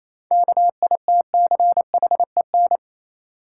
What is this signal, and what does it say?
Morse code, heard as a keyed tone, about 27 wpm, 690 Hz. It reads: KITCHEN
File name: Easy Words - Speed-Racing with Hints - 22wpm